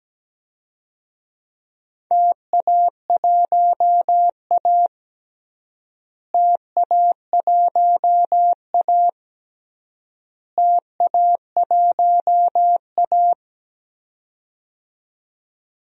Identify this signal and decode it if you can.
Morse code, heard as a keyed tone, about 17 wpm, 700 Hz. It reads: TA1A TA1A TA1A
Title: DXCC-100 Call Signs - 4 Characters - Repeated 3x - 17wpm